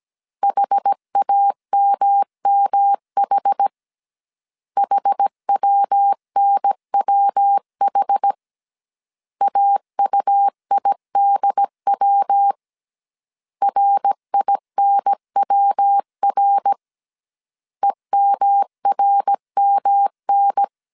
Morse code, heard as a keyed tone, about 17 wpm, 800 Hz. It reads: HAMMH HWNWH AUIDW RINWR EMRMN